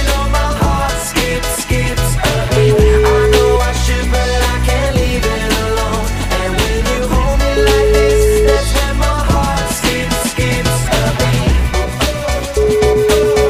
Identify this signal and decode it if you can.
Ringback tone (Europe, 425 Hz), with noise.